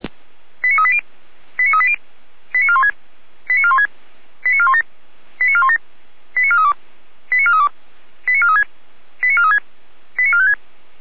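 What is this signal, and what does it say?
Five-tone selective call (ZVEI), heard as quick runs of five tones, tones of about 70 ms. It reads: 89380 89380 89416 89416 89417 89417 89432 89432 89436 89436 89456
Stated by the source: ZVEI five-tone call 89402 received with an SDR.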